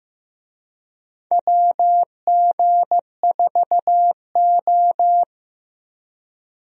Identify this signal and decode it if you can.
Morse code, heard as a keyed tone, about 15 wpm, 700 Hz. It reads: WG4O
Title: USA Call Signs - 4 Characters - Repeated 2x - 15wpm